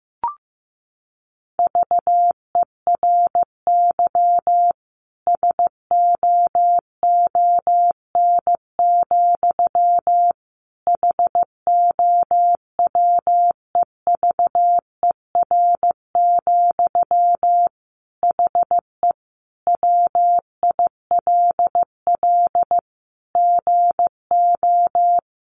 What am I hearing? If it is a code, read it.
Morse code, heard as a keyed tone, about 14 wpm, 700 Hz. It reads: VERY SOON, HOWEVER, HE WILL GO
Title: Sentences from Top 1400 Words 15wpm